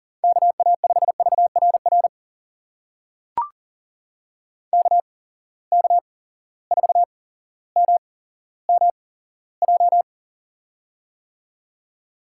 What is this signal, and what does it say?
Morse code, heard as a keyed tone, about 40 wpm, spaced as for 17 wpm, 700 Hz. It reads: KA5VRR KK4MMJ
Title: USA Call Signs - 6 Characters - 8x Character Spacing - 40wpm